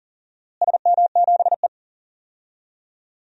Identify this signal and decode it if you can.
Morse code, heard as a keyed tone, about 40 wpm, 700 Hz. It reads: SM7E
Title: DXCC-100 Call Signs - 4 Characters - Repeated 3x - 40wpm